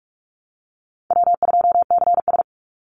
Morse code, heard as a keyed tone, about 44 wpm, 700 Hz. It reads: W2CS
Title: Call Signs Repeated 3x - 45wpm